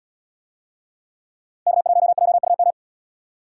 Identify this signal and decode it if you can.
Morse code, heard as a keyed tone, about 77 wpm, 700 Hz. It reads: K1JUM